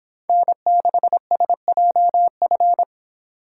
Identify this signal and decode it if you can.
Morse code, heard as a keyed tone, about 26 wpm, 700 Hz. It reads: N6SJF